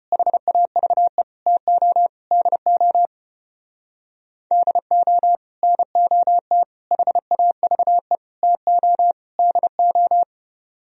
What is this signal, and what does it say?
Morse code, heard as a keyed tone, about 33 wpm, 700 Hz. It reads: HAVE TO DO DO NOT HAVE TO DO